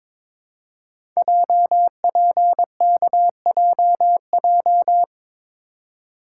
Morse code, heard as a keyed tone, about 22 wpm, 700 Hz. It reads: JPKJJ
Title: Code Groups - 5 Letter:Numbers - 22wpm